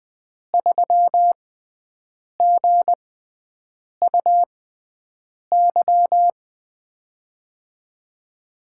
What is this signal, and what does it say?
Morse code, heard as a keyed tone, about 20 wpm, spaced as for 7 wpm, 700 Hz. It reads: 3GUY
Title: USA Call Signs - 6x Character Spacing - 20wpm